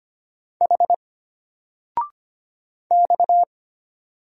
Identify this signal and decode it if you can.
Morse code, heard as a keyed tone, about 25 wpm, 700 Hz. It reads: H X
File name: Single-Letter-Number-25wpm